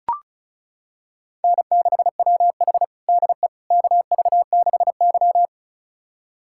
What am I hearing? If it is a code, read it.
Morse code, heard as a keyed tone, about 36 wpm, 700 Hz. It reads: N6WH DE KV6Y
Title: USA Callsign DE Callsign - 4 Characters - 35wpm